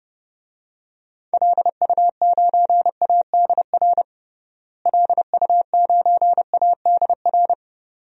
Morse code, heard as a keyed tone, about 30 wpm, 700 Hz. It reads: LU9ADR LU9ADR
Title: DXCC-100 Call Signs - 6 Characters - Repeated 2x - 30wpm